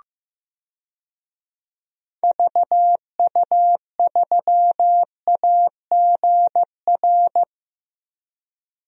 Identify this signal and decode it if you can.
Morse code, heard as a keyed tone, about 15 wpm, 700 Hz. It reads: VU3AGR
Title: DXCC-100 Call Signs - 6 Characters - 15wpm